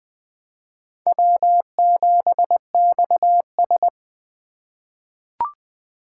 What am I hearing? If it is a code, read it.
Morse code, heard as a keyed tone, about 20 wpm, 700 Hz. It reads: W7XS
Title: USA Call Signs - 4 Characters - 2x Character Spacing - 20wpm